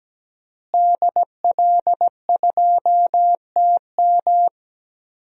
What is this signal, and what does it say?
Morse code, heard as a keyed tone, about 17 wpm, 700 Hz. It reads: DL2TM
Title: DXCC-100 Call Signs - 5 Characters - Repeated 3x - 17wpm